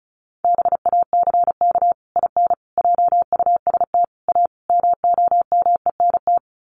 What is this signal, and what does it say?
Morse code, heard as a keyed tone, about 34 wpm, 700 Hz. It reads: BACK IN JUST A MOMENT
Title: Sets of 5 Words Encore 35wpm